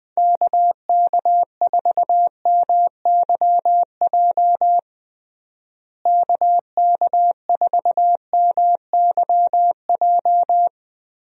Morse code, heard as a keyed tone, about 20 wpm, 700 Hz. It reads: KK4MYJ KK4MYJ
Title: USA Call Signs - 6 Characters - Repeated 2x - 20wpm